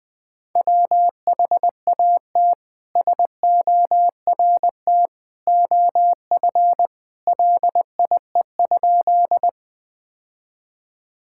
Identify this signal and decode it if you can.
Morse code, heard as a keyed tone, about 20 wpm, 700 Hz. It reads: WHAT SORT OF LIE?
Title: Sentences from Top 1600 Words 20wpm